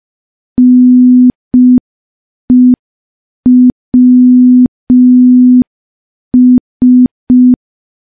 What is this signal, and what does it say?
Morse code, heard as a keyed tone, about 5 wpm, 250 Hz. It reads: NEWS
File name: morse / morse-training-05WPM